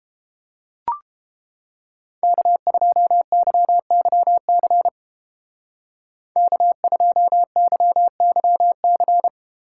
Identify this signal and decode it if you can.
Morse code, heard as a keyed tone, about 31 wpm, 700 Hz. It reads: K2YYC K2YYC